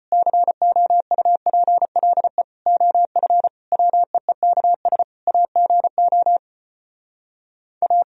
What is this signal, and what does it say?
Morse code, heard as a keyed tone, about 33 wpm, 700 Hz. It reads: COUPLE OF WEEKS AGO A